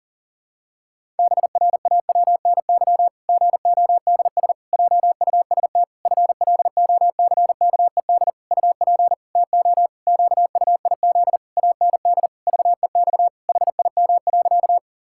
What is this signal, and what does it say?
Morse code, heard as a keyed tone, about 40 wpm, 700 Hz. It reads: BRAWNY GODS JUST FLOCKED UP TO QUIZ AND VEX HIM.
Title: Pangram Challenge v2 - 40wpm